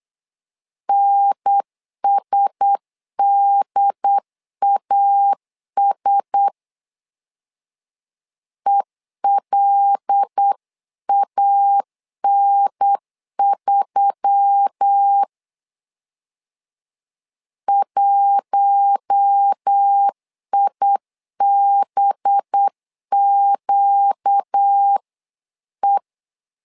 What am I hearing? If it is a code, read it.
Morse code, heard as a keyed tone, about 8 wpm, 800 Hz. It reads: NSDAS ELAN3 1IBQE
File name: LEZIONI / 5 LBVC / 5 LB 8wpm 2816